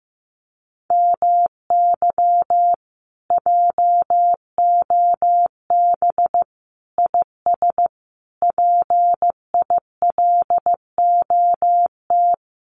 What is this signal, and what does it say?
Morse code, heard as a keyed tone, about 15 wpm, 700 Hz. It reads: MY JOB IS PILOT
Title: Common QSO Phrases Encore 15wpm